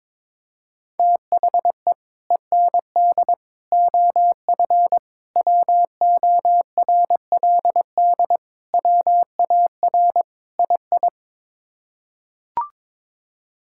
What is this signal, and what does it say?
Morse code, heard as a keyed tone, about 22 wpm, 700 Hz. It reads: THE END OF WORLD WAR II
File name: practice / 22wpm / Sets of 6 Words - 22wpm